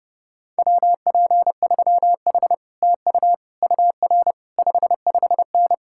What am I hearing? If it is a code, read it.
Morse code, heard as a keyed tone, about 33 wpm, 700 Hz. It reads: WP3H TU UR 55N